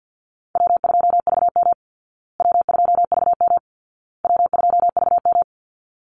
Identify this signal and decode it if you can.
Morse code, heard as a keyed tone, about 50 wpm, 700 Hz. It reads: W2VM W2VM W2VM
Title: Easy Call Signs Repeated 3x - 50wpm